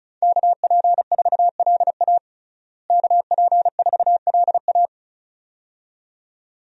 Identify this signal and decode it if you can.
Morse code, heard as a keyed tone, about 36 wpm, 700 Hz. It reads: KP4LA KP4LA
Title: USA Call Signs - 5 Characters - Repeated 2x - 35wpm